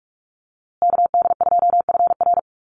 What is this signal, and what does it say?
Morse code, heard as a keyed tone, about 44 wpm, 690 Hz. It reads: KD2FR
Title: USA Call Signs Repeated 2x 45wpm